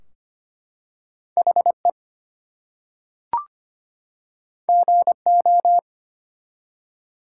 Morse code, heard as a keyed tone, about 25 wpm, 700 Hz. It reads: HE GO
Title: Two Letter Words - 25wpm